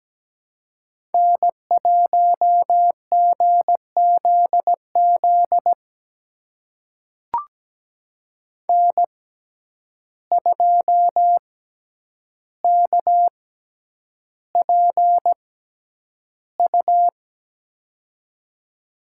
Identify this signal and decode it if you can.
Morse code, heard as a keyed tone, about 17 wpm, spaced as for 9 wpm, 700 Hz. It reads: N1GZZ N2KPU
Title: USA Call Signs - 5 Characters - 6x Character Spacing - 17wpm